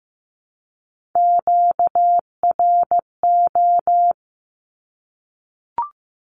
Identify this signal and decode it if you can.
Morse code, heard as a keyed tone, about 15 wpm, 700 Hz. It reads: QRO
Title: Q-Codes Repeated 3x 15wpm